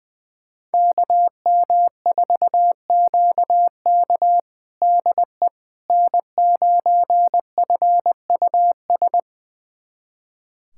Morse code, heard as a keyed tone, about 20 wpm, 700 Hz. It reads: KM4QK DE N9FUS
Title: USA Callsign DE Callsign - 5 Characters - 20wpm